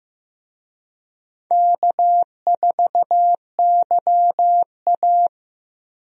Morse code, heard as a keyed tone, about 16 wpm, 700 Hz. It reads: K4YA
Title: USA Call Signs - 4 Characters - Repeated 2x - 15wpm